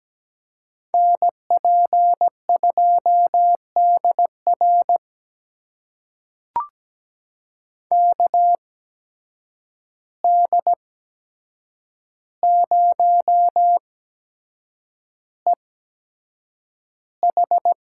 Morse code, heard as a keyed tone, about 17 wpm, spaced as for 7 wpm, 700 Hz. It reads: NP2DR KD0EH
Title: USA Call Signs - 5 Characters - 8x Character Spacing - 17wpm